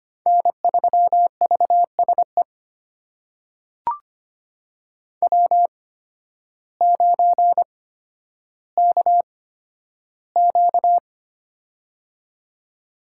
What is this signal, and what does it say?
Morse code, heard as a keyed tone, about 25 wpm, spaced as for 12 wpm, 700 Hz. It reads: N3VSE W9KQ
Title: USA Call Signs - 8x Character Spacing - 25wpm